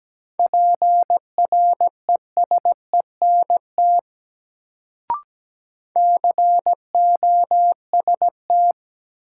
Morse code, heard as a keyed tone, about 17 wpm, 700 Hz. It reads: PRESENT COST
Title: Sentences from Top 800 Words - New Word Drill - 17wpm